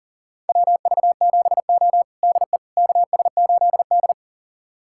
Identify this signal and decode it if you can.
Morse code, heard as a keyed tone, about 40 wpm, 700 Hz. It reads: WV7O DE KS8D
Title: USA Callsign DE Callsign - 4 Characters - 40wpm